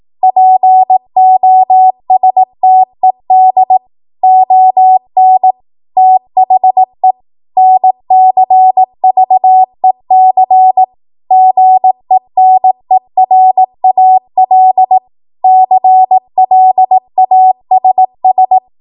Morse code, heard as a keyed tone, about 18 wpm, 750 Hz. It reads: POSTED ON THE NCVEC GENERAL CLASS